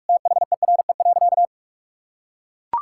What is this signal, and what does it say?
Morse code, heard as a keyed tone, about 43 wpm, 700 Hz. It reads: THERE.
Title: Sentences from Top 1600 Words 45wpm